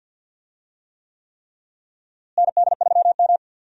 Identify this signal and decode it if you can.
Morse code, heard as a keyed tone, about 46 wpm, 700 Hz. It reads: ND3M
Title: USA Call Signs - 4 Characters - Repeated 3x - 50wpm